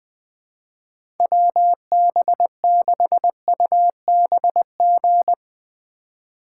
Morse code, heard as a keyed tone, about 20 wpm, 700 Hz. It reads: WB6UBG
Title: USA Call Signs - 6 Characters - 4x Character Spacing - 20wpm